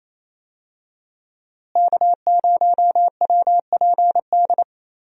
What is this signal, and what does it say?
Morse code, heard as a keyed tone, about 28 wpm, 700 Hz. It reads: K0WPD
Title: USA Call Signs - 5 Characters - Repeated 2x - 28wpm